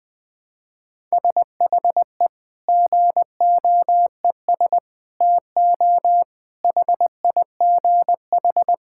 Morse code, heard as a keyed tone, about 21 wpm, 700 Hz. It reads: SHE GOES TO HIGH
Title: Sentences with 2-Word Phrases - Spoken Beginning - 20wpm